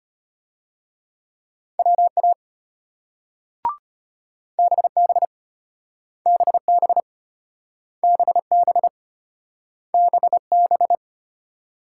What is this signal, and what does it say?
Morse code, heard as a keyed tone, about 34 wpm, 700 Hz. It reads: WA BB BB BB BB